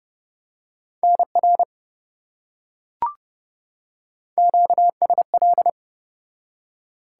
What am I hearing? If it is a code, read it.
Morse code, heard as a keyed tone, about 28 wpm, 700 Hz. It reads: NR QSL